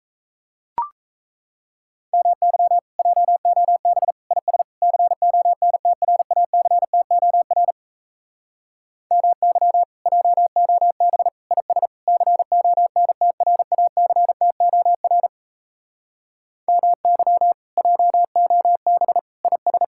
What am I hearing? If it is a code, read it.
Morse code, heard as a keyed tone, about 36 wpm, 700 Hz. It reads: MY JOB IS CONTRACTOR MY JOB IS CONTRACTOR MY JOB IS